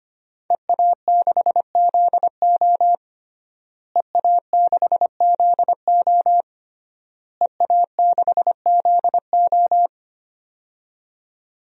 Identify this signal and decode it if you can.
Morse code, heard as a keyed tone, about 25 wpm, 700 Hz. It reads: EA6ZO EA6ZO EA6ZO
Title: DXCC-100 Call Signs - 5 Characters - Repeated 3x - 25wpm